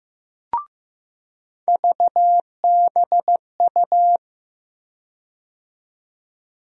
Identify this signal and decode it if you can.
Morse code, heard as a keyed tone, about 15 wpm, 700 Hz. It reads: VBU
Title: USA Call Sign Postfixes - 3 Letters 15wpm